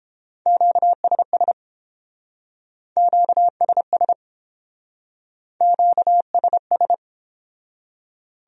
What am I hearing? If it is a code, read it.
Morse code, heard as a keyed tone, about 29 wpm, 700 Hz. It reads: QSS QSS QSS